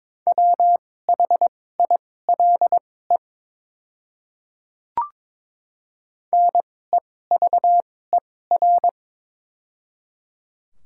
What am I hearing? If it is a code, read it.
Morse code, heard as a keyed tone, about 22 wpm, spaced as for 16 wpm, 700 Hz. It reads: WHILE NEVER
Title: 2x Character Spacing - Top 300 Words - 22wpm